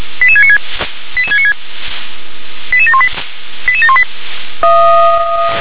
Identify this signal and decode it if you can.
Five-tone selective call (ZVEI), heard as quick runs of five tones, tones of about 80 ms. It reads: 88696 88696 88718 88718